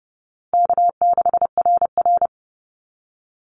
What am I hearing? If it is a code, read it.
Morse code, heard as a keyed tone, about 30 wpm, 700 Hz. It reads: K6RR
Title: Call Signs Repeated 2x - 30wpm